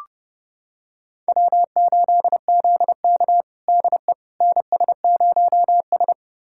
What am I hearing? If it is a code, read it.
Morse code, heard as a keyed tone, about 30 wpm, 700 Hz. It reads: W8ZK DE NS0S
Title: USA Callsign DE Callsign - 4 Characters - 30wpm